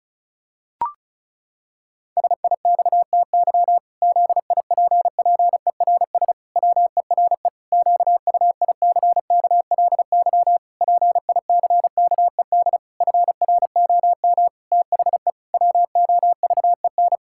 Morse code, heard as a keyed tone, about 36 wpm, 700 Hz. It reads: SIXTY ZIPPERS WERE QUICKLY PICKED FROM THE WOVEN
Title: Pangram Challenge v2 - 35wpm